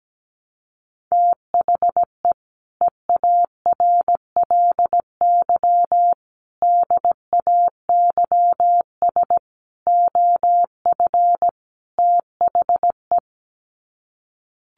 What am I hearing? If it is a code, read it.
Morse code, heard as a keyed tone, about 17 wpm, 700 Hz. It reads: THE EARLY DAYS OF THE
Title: Sets of 5 Words Encore 17wpm